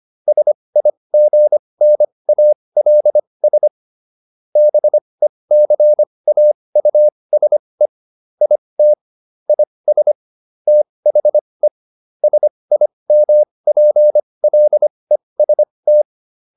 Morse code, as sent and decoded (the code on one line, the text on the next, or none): ... .. --. -. .- .-.. ... / -... -.-. .- ..- ... / .. - / .. ... / - .... / ... .. -- .--. .-.. ... -
SIGNALS BECAUSE IT IS THE SIMPLEST